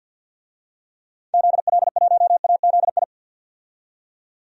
Morse code, heard as a keyed tone, about 52 wpm, 700 Hz. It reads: ZL1AZI
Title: DXCC-100 Call Signs - 6 Characters - 50wpm